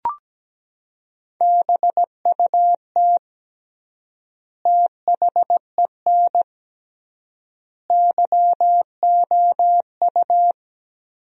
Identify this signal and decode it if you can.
Morse code, heard as a keyed tone, about 17 wpm, 700 Hz. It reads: BUT THEN YOU
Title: 3x Word-Spacing - Sentences from Top 300 Words - 17wpm